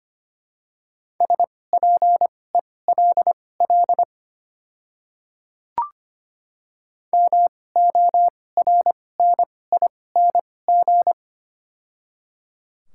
Morse code, heard as a keyed tone, about 25 wpm, spaced as for 18 wpm, 700 Hz. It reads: SPELL MORNING
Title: 2x Character Spacing - Top 500 Words - 25wpm